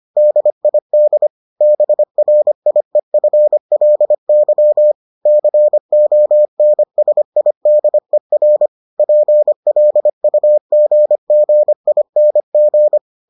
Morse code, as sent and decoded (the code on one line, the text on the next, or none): -.. .. -.. / -... .-. .. ..-. .-.. -.-- / -.-. --- -. ... .. -.. .-. / .--. .-.. ..- --. --. .. -. --.
DID BRIEFLY CONSIDER PLUGGING